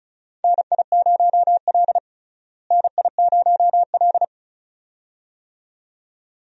Morse code, heard as a keyed tone, about 36 wpm, 700 Hz. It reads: NI0L NI0L
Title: USA Call Signs - 4 Characters - Repeated 2x - 35wpm